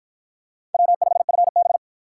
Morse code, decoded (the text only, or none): W5FB